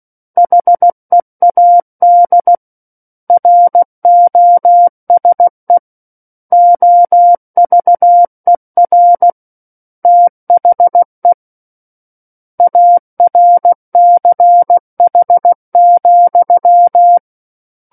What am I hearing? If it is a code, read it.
Morse code, heard as a keyed tone, about 16 wpm, 700 Hz. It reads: HEAD ROSE OVER THE ARCH,
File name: Book One / 07-Chapter 7 - 16 WPM